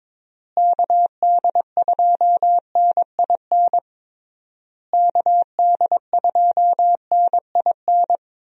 Morse code, heard as a keyed tone, about 22 wpm, 700 Hz. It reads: KD2NIN KD2NIN
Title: USA Call Signs - 6 Characters - Repeated 3x - 22wpm